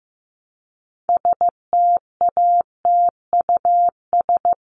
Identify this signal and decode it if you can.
Morse code, heard as a keyed tone, about 15 wpm, 700 Hz. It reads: STATUS
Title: Top 1251-1500 Words 15wpm